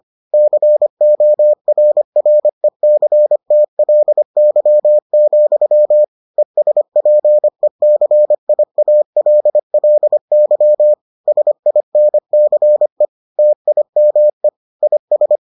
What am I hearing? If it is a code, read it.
Morse code, heard as a keyed tone, about 25 wpm, 600 Hz. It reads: CORRECTLY, ESPECIALLY SINCE TIME IS